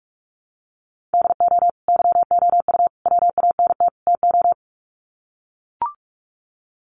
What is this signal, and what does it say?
Morse code, heard as a keyed tone, about 45 wpm, 700 Hz. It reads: DO YOU WANT TO
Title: Sets of 4 Words 45wpm